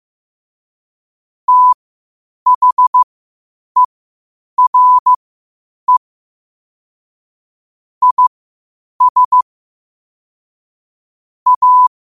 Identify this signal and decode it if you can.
Morse code, heard as a keyed tone, about 15 wpm, spaced as for 8 wpm, 1000 Hz. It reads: THERE IS A
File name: qotdfiles / 8wpm